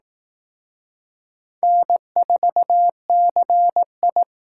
Morse code, heard as a keyed tone, about 18 wpm, 700 Hz. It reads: N4CI
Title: Speed Racing - USA 1x2 Call Signs 15wpm